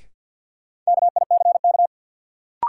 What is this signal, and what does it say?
Morse code, heard as a keyed tone, about 53 wpm, 700 Hz. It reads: KICK
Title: Sentences from Top 1300 Words - New Word Drill - 50wpm